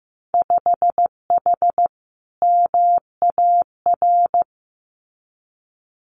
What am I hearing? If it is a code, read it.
Morse code, heard as a keyed tone, about 15 wpm, 700 Hz. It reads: 5H MAR
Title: Winter Field Day Exchanges Encore 15wpm